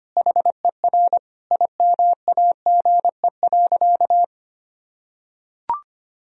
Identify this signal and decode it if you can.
Morse code, heard as a keyed tone, about 25 wpm, 700 Hz. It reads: HER IMAGE.
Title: Sentences from Top 1000 Words - Review Part 4 - 25wpm